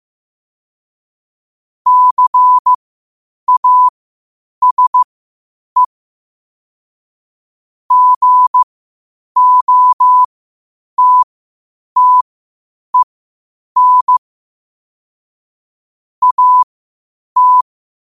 Morse code, heard as a keyed tone, about 15 wpm, spaced as for 8 wpm, 1000 Hz. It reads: CASE GOTTEN AT
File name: qotdfiles / 8wpm